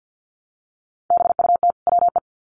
Morse code, heard as a keyed tone, about 50 wpm, 700 Hz. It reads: BUT WE
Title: Sets of 2 Words 50wpm